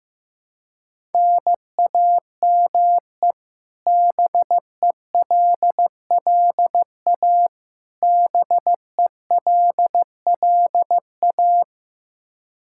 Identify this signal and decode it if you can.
Morse code, heard as a keyed tone, about 16 wpm, 700 Hz. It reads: NAME BELLA BELLA
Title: Practice QSO - Exchange 1 - Beginning - 15wpm